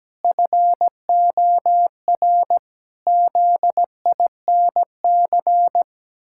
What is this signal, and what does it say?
Morse code, heard as a keyed tone, about 17 wpm, 700 Hz. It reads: FOR ZINC
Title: Pangram Challenge v2 - 17wpm